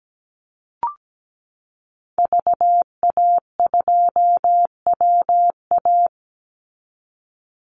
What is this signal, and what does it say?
Morse code, heard as a keyed tone, about 17 wpm, 700 Hz. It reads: VA2WA
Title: Call Signs Repeated 3x - 17wpm